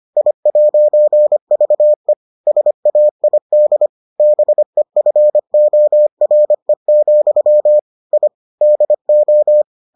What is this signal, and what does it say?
Morse code, heard as a keyed tone, about 25 wpm, 600 Hz. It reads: I'VE SAID BEFORE, I DO